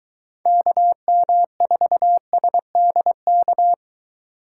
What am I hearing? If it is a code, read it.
Morse code, heard as a keyed tone, about 23 wpm, 700 Hz. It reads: KM4SDK